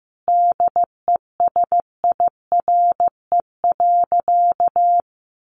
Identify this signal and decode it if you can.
Morse code, heard as a keyed tone, about 15 wpm, 700 Hz. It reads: DESIRE.
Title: Sentences from Top 1300 Words 15wpm